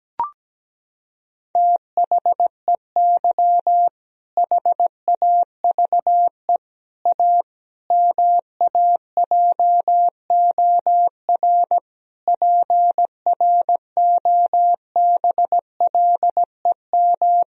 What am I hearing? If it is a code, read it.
Morse code, heard as a keyed tone, about 17 wpm, 700 Hz. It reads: THEY HAVE A MAJOR PROBLEM
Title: Sentences from Top 1000 Words - Review Part 4 - 17wpm